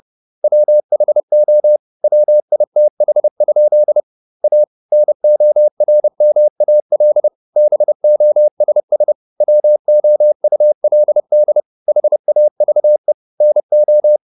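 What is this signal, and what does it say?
Morse code, heard as a keyed tone, about 30 wpm, 600 Hz. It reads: WHO WITH? A NORMAL BOSS WOULD HAVE NO
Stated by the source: Morse code at 30 wpm, the novel Never by Ken Follett in English, from CW med Gnister.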